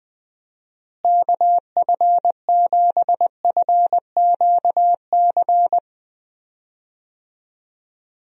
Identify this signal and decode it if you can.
Morse code, heard as a keyed tone, about 20 wpm, 700 Hz. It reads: KF7FQC
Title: USA Call Signs - 6 Characters - Repeated 2x - 20wpm